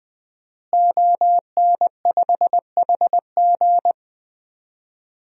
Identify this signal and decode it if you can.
Morse code, heard as a keyed tone, about 20 wpm, 700 Hz. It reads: ON5HG
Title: DXCC-100 Call Signs - 5 Characters - Repeated 3x - 20wpm